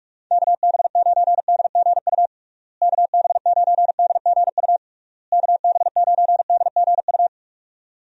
Morse code, heard as a keyed tone, about 46 wpm, 700 Hz. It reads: KB9DGU KB9DGU KB9DGU
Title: USA Call Signs - 6 Characters - Repeated 3x - 45wpm